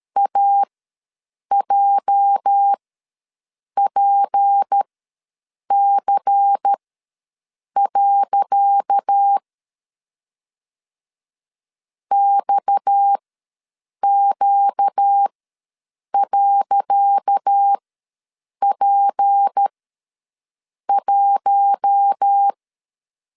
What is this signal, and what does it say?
Morse code, heard as a keyed tone, about 13 wpm, spaced as for 7 wpm, 800 Hz. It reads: AJPC. XQ.P1
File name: LEZIONI / 10 SIGNS / 10C DOT ws 12wpm 3548